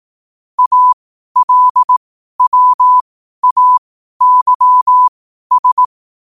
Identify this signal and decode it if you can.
Morse code, heard as a keyed tone, about 18 wpm, spaced as for 12 wpm, 1000 Hz. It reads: ALWAYS
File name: qotdfiles / 12wpm